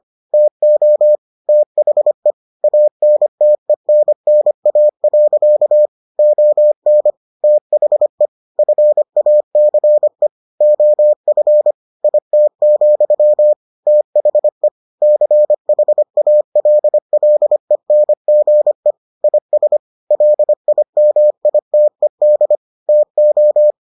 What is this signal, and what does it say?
Morse code, heard as a keyed tone, about 25 wpm, 600 Hz. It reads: TO THE ANTENNA. ON THE FACE OF IT, THE CHALLENGE IS LIMITED TO